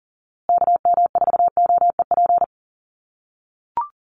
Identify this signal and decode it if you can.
Morse code, heard as a keyed tone, about 40 wpm, 700 Hz. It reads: KM4OEP
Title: USA Call Signs Repeated 3x 40wpm